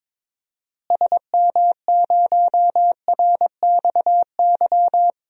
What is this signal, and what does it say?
Morse code, heard as a keyed tone, about 22 wpm, 700 Hz. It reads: SM0RXY
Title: DXCC-100 Call Signs - 6 Characters - Repeated 3x - 22wpm